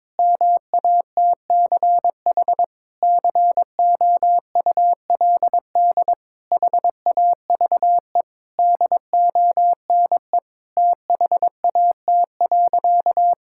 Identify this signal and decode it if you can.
Morse code, heard as a keyed tone, about 22 wpm, 700 Hz. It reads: MATCH COULD HAVE DONE THAT.
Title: Sentences from Top 1600 Words 22wpm